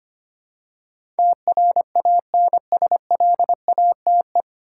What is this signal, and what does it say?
Morse code, heard as a keyed tone, about 25 wpm, 700 Hz. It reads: TRANSLATE
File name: practice / 25wpm / 9 Letter Words - First 3 Letters Given - 25wpm